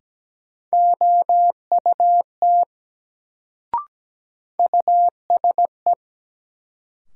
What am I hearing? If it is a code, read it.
Morse code, heard as a keyed tone, about 17 wpm, 700 Hz. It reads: OUT USE
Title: Sentences from Top 100 Words - New Word Drill - 17wpm